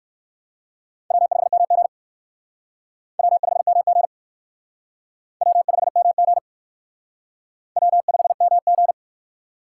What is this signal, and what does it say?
Morse code, heard as a keyed tone, about 55 wpm, 700 Hz. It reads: W5MG W5MG W5MG W5MG
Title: Speed Racing - USA 1x2 Call Signs 45wpm